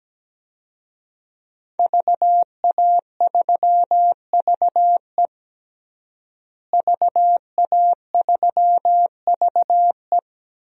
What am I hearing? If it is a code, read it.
Morse code, heard as a keyed tone, about 17 wpm, 700 Hz. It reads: VA3VE VA3VE
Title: DXCC-100 Call Signs - 5 Characters - Repeated 3x - 17wpm